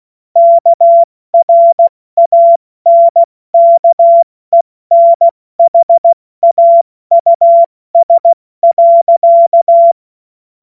Morse code, heard as a keyed tone, about 16 wpm, 680 Hz. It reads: KRANKENHAUS.
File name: Morsekurs / Witze / Witz 27 Mai 2025 70 F80